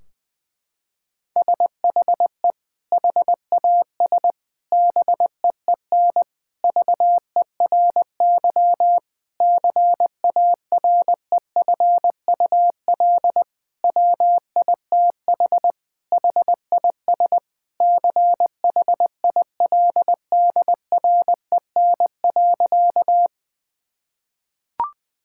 Morse code, heard as a keyed tone, about 20 wpm, 700 Hz. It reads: SHE HAS BEEN VERY CAREFUL WITH HIS CHILDREN.
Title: Sentences from Top 1600 Words 20wpm